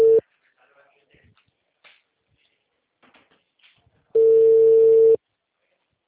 Ringback tone (China, 450 Hz).